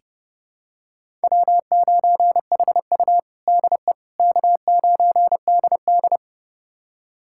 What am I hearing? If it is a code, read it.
Morse code, heard as a keyed tone, about 30 wpm, 700 Hz. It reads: W9HU DE K9DD
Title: USA Callsign DE Callsign - 4 Characters - 30wpm